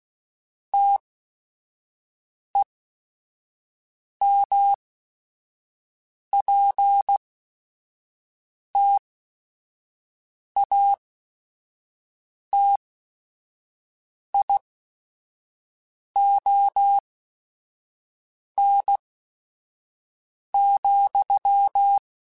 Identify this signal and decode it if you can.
Morse code, heard as a keyed tone, about 16 wpm, spaced as for 5 wpm, 800 Hz. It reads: TEMPTATION,
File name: Chuckles / Chuckles-mp3s / Temptation @5